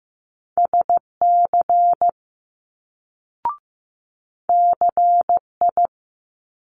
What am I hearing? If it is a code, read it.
Morse code, heard as a keyed tone, about 15 wpm, 700 Hz. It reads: SC CI
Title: Multi-letter Phonemes Quick Fire 15wpm